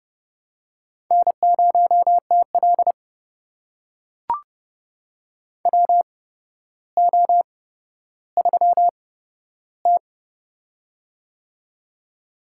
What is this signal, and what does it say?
Morse code, heard as a keyed tone, about 30 wpm, spaced as for 13 wpm, 700 Hz. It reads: N0TL WO3T
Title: USA Call Signs - 4 Characters - 8x Character Spacing - 30wpm